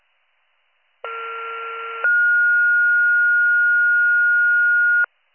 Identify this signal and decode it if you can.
Two-tone page, heard as a one-second tone, then a three-second tone, about 1370 then 1430 Hz. It is heard from 1.1 to 5.1 s.